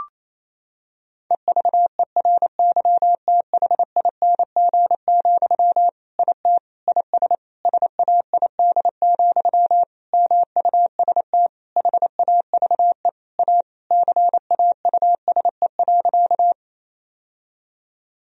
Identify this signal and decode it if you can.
Morse code, heard as a keyed tone, about 27 wpm, 700 Hz. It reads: EVERYTHING, IT IS SAID, MUST HAVE A CAUSE.
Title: Sentences from Top 1000 Words - Review Part 1 - 28wpm